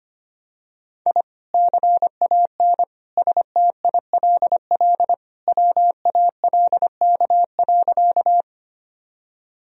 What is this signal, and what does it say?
Morse code, heard as a keyed tone, about 25 wpm, 700 Hz. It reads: I CAN STILL WALK.
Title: Sentences from Top 1000 Words - Review Part 1 - 25wpm